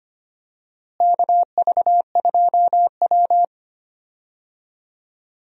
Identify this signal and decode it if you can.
Morse code, heard as a keyed tone, about 25 wpm, 700 Hz. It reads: KV2W
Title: USA Call Signs - 4 Characters - Repeated 2x - 25wpm